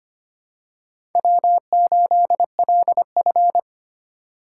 Morse code, heard as a keyed tone, about 25 wpm, 700 Hz. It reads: W8LF
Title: USA Call Signs - 4 Characters - 3x Character Spacing - 25wpm